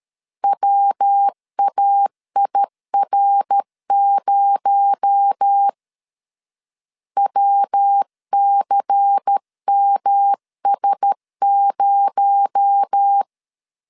Morse code, heard as a keyed tone, about 13 wpm, 800 Hz. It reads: WAIR0 WCMS0
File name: CW / DET / 1 6 1500C S C 12wpm 3533